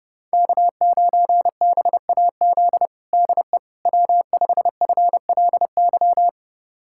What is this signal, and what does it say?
Morse code, heard as a keyed tone, about 30 wpm, 700 Hz. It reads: K9BAZ DE W5FLY
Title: USA Callsign DE Callsign - 5 Characters - 30wpm